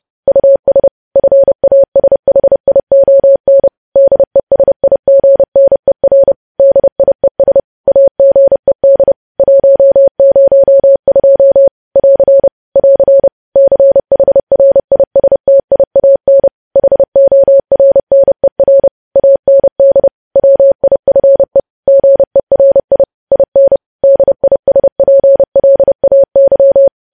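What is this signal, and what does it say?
Morse code, heard as a keyed tone, about 30 wpm, 550 Hz. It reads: US FASHION DESIGNER DIES AGED 102 <AR> <AR> CHRISTIAN HORNER AND WIFE GERI IN DISPLAY
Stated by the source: Morse code practice of news headlines at 30.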